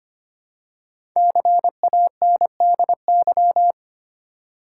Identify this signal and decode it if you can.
Morse code, heard as a keyed tone, about 25 wpm, 700 Hz. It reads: CANDY
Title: Five Letter Words - 25wpm